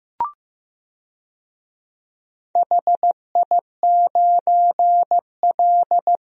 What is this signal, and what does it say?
Morse code, heard as a keyed tone, about 15 wpm, 700 Hz. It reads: HI9L
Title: DXCC-100 Call Signs - 4 Characters - Repeated 2x - 15wpm